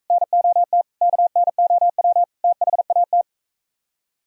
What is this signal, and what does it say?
Morse code, heard as a keyed tone, about 45 wpm, 700 Hz. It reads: NOT KNOW THAT